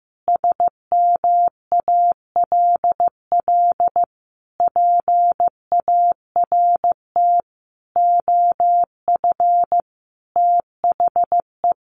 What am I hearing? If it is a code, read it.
Morse code, heard as a keyed tone, about 14 wpm, 700 Hz. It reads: SMALL PART OF THE
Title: Sentences from Top 1500 Words 15wpm